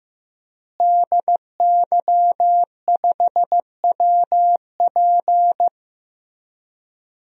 Morse code, heard as a keyed tone, about 16 wpm, 700 Hz. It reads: DY5WP
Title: DXCC-100 Call Signs - 5 Characters - Repeated 3x - 15wpm